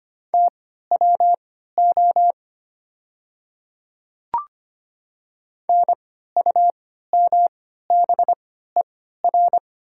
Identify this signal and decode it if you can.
Morse code, heard as a keyed tone, about 25 wpm, spaced as for 14 wpm, 700 Hz. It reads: TWO NUMBER
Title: 3x Character Spacing - Top 100 Words - 25wpm